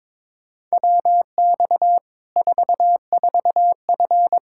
Morse code, heard as a keyed tone, about 22 wpm, 700 Hz. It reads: WX 44F
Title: K1USN SST QSO Practice 22wpm